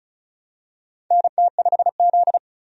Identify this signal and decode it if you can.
Morse code, heard as a keyed tone, about 36 wpm, 700 Hz. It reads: NT5Z
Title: USA Call Signs - 4 Characters - 35wpm